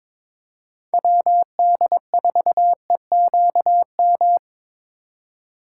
Morse code, heard as a keyed tone, about 25 wpm, 700 Hz. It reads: WD4EQM
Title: Speed Racing - USA 2x3 Call Signs 22wpm